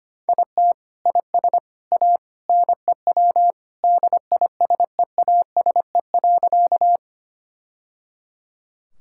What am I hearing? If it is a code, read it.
Morse code, heard as a keyed tone, about 25 wpm, 700 Hz. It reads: IT IS A NEW DISEASE.
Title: Sentences from Top 1000 Words - Review Part 6 - 25wpm